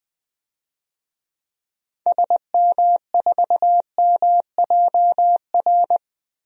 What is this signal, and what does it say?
Morse code, heard as a keyed tone, about 21 wpm, 700 Hz. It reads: SM4MJR
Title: DXCC-100 Call Signs - 6 Characters - Repeated 2x - 20wpm